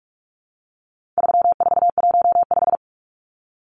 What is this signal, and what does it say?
Morse code, heard as a keyed tone, about 44 wpm, 700 Hz. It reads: 3415
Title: Numbers - 4-Digit 45wpm